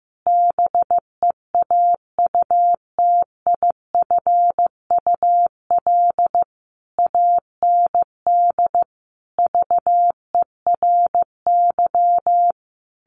Morse code, heard as a keyed tone, about 14 wpm, 700 Hz. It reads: BEAUTIFUL AND VERY
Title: A Sound Of Thunder 15wpm